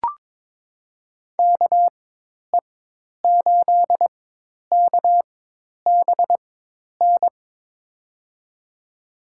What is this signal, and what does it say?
Morse code, heard as a keyed tone, about 22 wpm, spaced as for 10 wpm, 700 Hz. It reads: KE8KBN
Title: USA Call Signs - 6 Characters - 4x Character Spacing - 22wpm